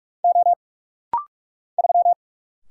Morse code, heard as a keyed tone, about 44 wpm, 700 Hz. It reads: O 3
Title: Single Letter-Number Alt - Rapid-Fire 45wpm